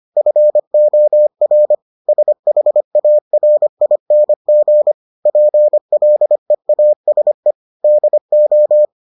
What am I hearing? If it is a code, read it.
Morse code, heard as a keyed tone, about 25 wpm, 600 Hz. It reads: FOR SHARING PLEASE DO